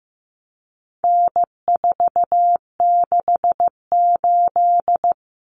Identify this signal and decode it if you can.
Morse code, heard as a keyed tone, about 15 wpm, 700 Hz. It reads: N468
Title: Common Cut Numbers - 4-Digit 15wpm